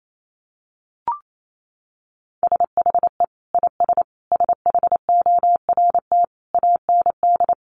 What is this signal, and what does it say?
Morse code, heard as a keyed tone, about 28 wpm, 700 Hz. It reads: SHE IS SHORT AND
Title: Sentences from Top 1200 Words 28wpm